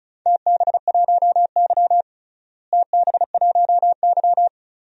Morse code, heard as a keyed tone, about 36 wpm, 690 Hz. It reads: TB1Y TB1Y
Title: DXCC-100 Call Signs - 4 Characters - Repeated 3x - 35wpm